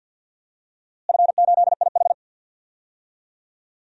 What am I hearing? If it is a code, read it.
Morse code, heard as a keyed tone, about 52 wpm, 700 Hz. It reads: F8IH
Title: DXCC-100 Call Signs - 4 Characters - Repeated 2x - 50wpm